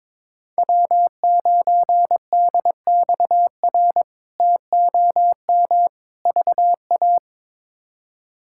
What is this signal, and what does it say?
Morse code, heard as a keyed tone, about 21 wpm, 700 Hz. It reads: W9DXR TOM VA
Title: K1USN SST QSO Practice - 22wpm